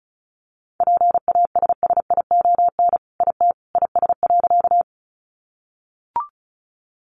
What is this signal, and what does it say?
Morse code, heard as a keyed tone, about 34 wpm, 700 Hz. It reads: PASSION IT IS.
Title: Sentences from Top 1300 Words 35wpm